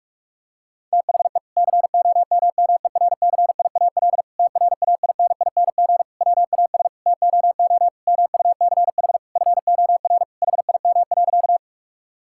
Morse code, heard as a keyed tone, about 42 wpm, 700 Hz. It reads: THE COMMERCIAL TRAINING WAS TOO MUCH FOR HIM.
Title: Sentences from Top 1000 Words - Review Part 5 - 45wpm